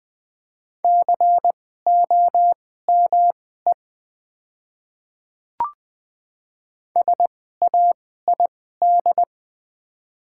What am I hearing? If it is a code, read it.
Morse code, heard as a keyed tone, about 20 wpm, spaced as for 14 wpm, 700 Hz. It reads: COME SAID
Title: 2x Character Spacing - Top 200 Words - 20wpm